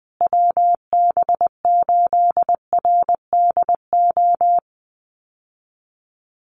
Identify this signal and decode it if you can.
Morse code, heard as a keyed tone, about 20 wpm, 700 Hz. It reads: WB8RDO
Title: USA Call Signs Repeated 2x 20wpm